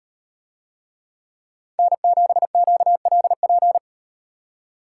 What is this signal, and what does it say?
Morse code, heard as a keyed tone, about 35 wpm, 700 Hz. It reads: N7QLP